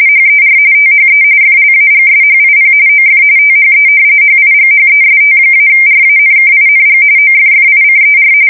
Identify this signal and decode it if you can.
RTTY, about 45.45 baud. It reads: HER THE FLAGSHIP OF THE HIGHEST-RANKING OFFICER
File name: fsk170 / battleshipnj